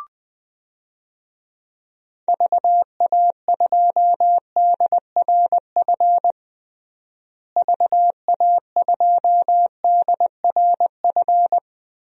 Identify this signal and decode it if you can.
Morse code, heard as a keyed tone, about 21 wpm, 700 Hz. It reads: VA2DRF VA2DRF
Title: DXCC-100 Call Signs - 6 Characters - Repeated 3x - 20wpm